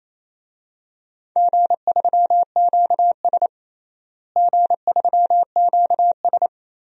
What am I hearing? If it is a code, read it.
Morse code, heard as a keyed tone, about 28 wpm, 700 Hz. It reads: G3QS G3QS
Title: DXCC-100 Call Signs - 4 Characters - Repeated 2x - 28wpm